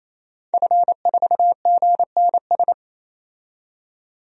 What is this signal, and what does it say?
Morse code, heard as a keyed tone, about 28 wpm, 700 Hz. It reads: F4GNS
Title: DXCC-100 Call Signs - 5 Characters - 28wpm